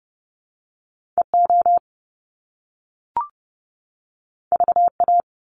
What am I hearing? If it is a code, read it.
Morse code, heard as a keyed tone, about 30 wpm, 700 Hz. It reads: EO VA
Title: Top Two Letter Combinations in Words 30wpm